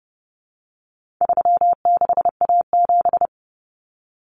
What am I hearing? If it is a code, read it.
Morse code, heard as a keyed tone, about 30 wpm, 700 Hz. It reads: 36A7
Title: Common Cut Numbers - 4-Digit 30wpm